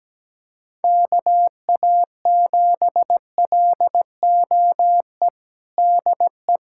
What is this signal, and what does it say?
Morse code, heard as a keyed tone, about 17 wpm, 700 Hz. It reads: KA7LOE DE
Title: USA Callsign DE Callsign - 6 Characters - 17wpm